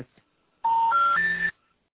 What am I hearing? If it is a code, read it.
Special information tone.